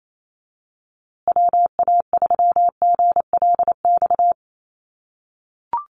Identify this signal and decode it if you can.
Morse code, heard as a keyed tone, about 28 wpm, 700 Hz. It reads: WA3GLX
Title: USA Call Signs Repeated 2x 28wpm